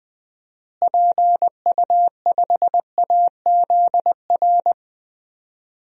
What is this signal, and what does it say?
Morse code, heard as a keyed tone, about 20 wpm, 700 Hz. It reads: PU5AZR
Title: DXCC-100 Call Signs - 6 Characters - Repeated 3x - 20wpm